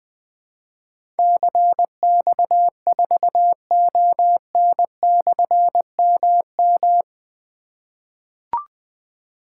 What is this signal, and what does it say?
Morse code, heard as a keyed tone, about 20 wpm, 700 Hz. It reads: CX4ON/MM
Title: Difficult Call Signs 20wpm